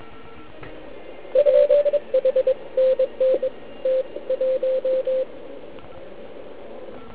Morse code, heard as a keyed tone, about 23 wpm, 520 Hz. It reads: R H NN T 1
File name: Archiv / CUC149 / OK1IF CUC149